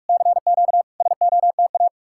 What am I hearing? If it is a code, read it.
Morse code, heard as a keyed tone, about 59 wpm, spaced as for 51 wpm, 700 Hz. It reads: CQ SOTA
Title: Speed-Racing Common QSO Phrases - 45wpm